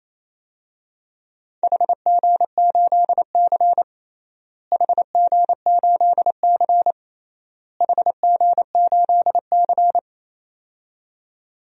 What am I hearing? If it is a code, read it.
Morse code, heard as a keyed tone, about 28 wpm, 700 Hz. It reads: HG8C HG8C HG8C
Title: DXCC-100 Call Signs - 4 Characters - Repeated 3x - 28wpm